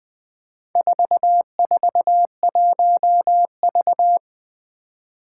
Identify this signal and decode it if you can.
Morse code, heard as a keyed tone, about 21 wpm, 700 Hz. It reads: 441V
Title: Four Challenging Letter-Number Combinations 20wpm